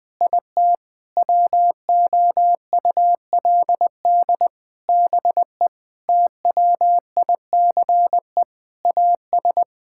Morse code, heard as a keyed tone, about 20 wpm, 700 Hz. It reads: IT WOULD BE TWICE AS
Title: Sentences from Top 1600 Words 20wpm